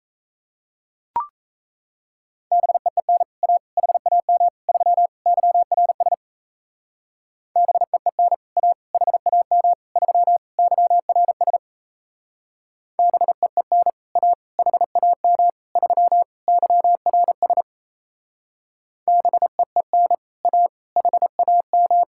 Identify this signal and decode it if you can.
Morse code, heard as a keyed tone, about 36 wpm, 700 Hz. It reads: BEEN A HAM 3 YRS BEEN A HAM 3 YRS BEEN A HAM 3 YRS BEEN A HAM